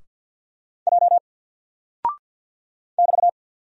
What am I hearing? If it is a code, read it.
Morse code, heard as a keyed tone, about 48 wpm, 700 Hz. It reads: J <BT>
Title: Single Character (Letter:Number:Punctuation:Prosign) Alt 50wpm